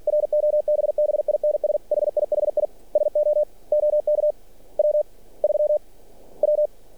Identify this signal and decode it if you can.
Morse code, heard as a keyed tone, about 46 wpm, spaced as for 35 wpm, 600 Hz. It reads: ROBBINS HIHI SO OK W3W
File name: archived / CRSnet